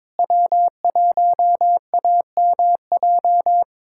Morse code, heard as a keyed tone, about 22 wpm, 700 Hz. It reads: W1AMJ
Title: USA Callsign DE Callsign - 5 Characters - 22wpm